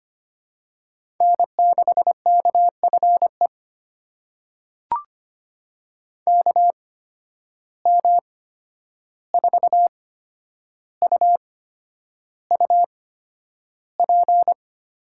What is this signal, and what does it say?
Morse code, heard as a keyed tone, about 25 wpm, spaced as for 10 wpm, 700 Hz. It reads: N6KFE KM4UUP
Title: USA Call Signs - 8x Character Spacing - 25wpm